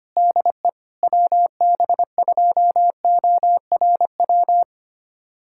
Morse code, heard as a keyed tone, about 25 wpm, 700 Hz. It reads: DE WB2ORW
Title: USA Callsign DE Callsign 25wpm